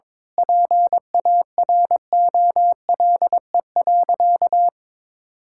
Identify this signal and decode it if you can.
Morse code, heard as a keyed tone, about 22 wpm, 700 Hz. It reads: PAROLE.
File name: practice / 22wpm / Sentences with 8-Word Phrases - 22wpm